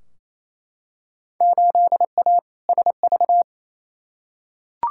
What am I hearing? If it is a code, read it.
Morse code, heard as a keyed tone, about 28 wpm, 700 Hz. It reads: 8A SV
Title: Field Day Exchanges Encore - 28wpm